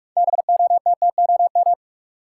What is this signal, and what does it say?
Morse code, heard as a keyed tone, about 42 wpm, 700 Hz. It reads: BOTTOM
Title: Six Letter Words - 45wpm